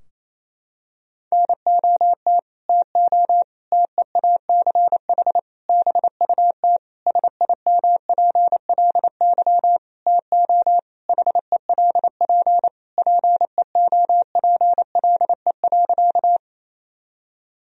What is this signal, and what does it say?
Morse code, heard as a keyed tone, about 28 wpm, 700 Hz. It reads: NOT TO TEACH BUT SIMPLY TO HELP PEOPLE.
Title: Sentences from Top 1000 Words - Review Part 4 - 28wpm